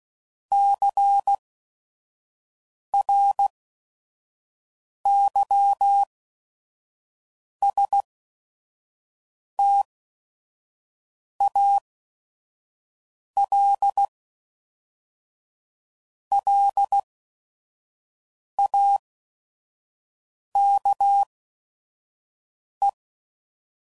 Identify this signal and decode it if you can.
Morse code, heard as a keyed tone, about 16 wpm, spaced as for 5 wpm, 800 Hz. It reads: CRYSTALLAKE
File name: CW Code Groups @5WPM / fromSM's / Cities